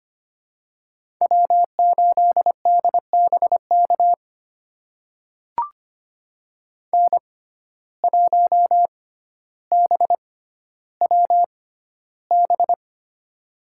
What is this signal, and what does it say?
Morse code, heard as a keyed tone, about 25 wpm, spaced as for 13 wpm, 700 Hz. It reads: W8DBK N1BWB
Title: USA Call Signs - 5 Characters - 6x Character Spacing - 25wpm